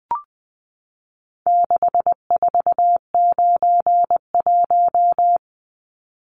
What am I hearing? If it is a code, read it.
Morse code, heard as a keyed tone, about 20 wpm, 700 Hz. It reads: 6491
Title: Numbers - 4-Digit 20wpm